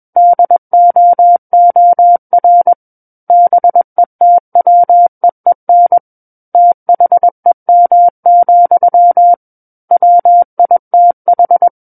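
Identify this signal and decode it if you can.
Morse code, heard as a keyed tone, about 21 wpm, 700 Hz. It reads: DOOR BETWEEN THEM, WITH
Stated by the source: Morse code practice reading of a book chapter at 21 words per minute.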